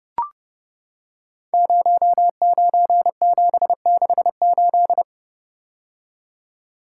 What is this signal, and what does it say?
Morse code, heard as a keyed tone, about 30 wpm, 700 Hz. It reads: 09768
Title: Code Groups - 5 Letter:Numbers - 30wpm